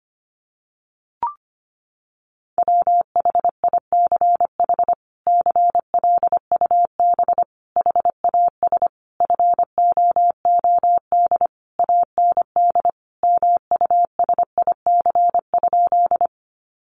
Morse code, heard as a keyed tone, about 25 wpm, 700 Hz. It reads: WHICH CLUB HAS FOOD AND MUSIC?
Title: Sentences from Top 1200 Words 25wpm